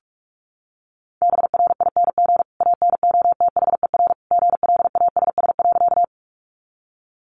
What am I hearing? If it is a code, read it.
Morse code, heard as a keyed tone, about 46 wpm, 700 Hz. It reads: BRING ANOTHER GLASS.
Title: Sentences from Top 1300 Words 45wpm